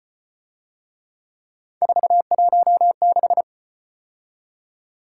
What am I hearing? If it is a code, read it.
Morse code, heard as a keyed tone, about 33 wpm, 700 Hz. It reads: 416